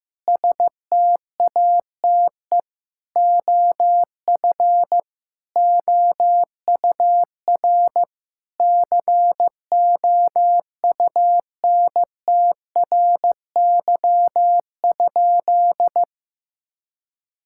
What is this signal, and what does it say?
Morse code, heard as a keyed tone, about 16 wpm, 700 Hz. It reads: STATE OF OUR COUNTRY?
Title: Sentences from Top 1000 Words - Review Part 5 - 15wpm